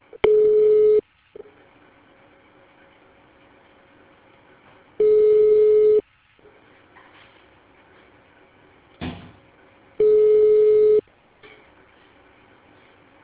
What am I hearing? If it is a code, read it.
Ringback tone (Europe, 425 Hz).